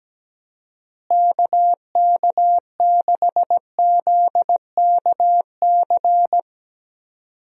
Morse code, heard as a keyed tone, about 17 wpm, 700 Hz. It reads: KK6ZKC
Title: USA Call Signs - 6 Characters - Repeated 3x - 17wpm